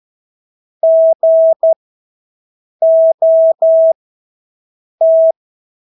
Morse code, heard as a keyed tone, about 12 wpm, spaced as for 6 wpm, 650 Hz. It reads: GOT
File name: Engelske ord 12-6 wpm